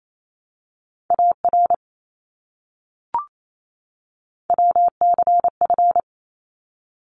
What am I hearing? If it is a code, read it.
Morse code, heard as a keyed tone, about 28 wpm, 700 Hz. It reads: AR WCF
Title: ARRL Sections 28wpm